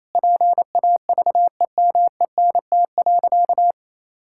Morse code, heard as a keyed tone, about 27 wpm, 700 Hz. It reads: PAVEMENT.
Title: Pangram Challenge v2 - 28wpm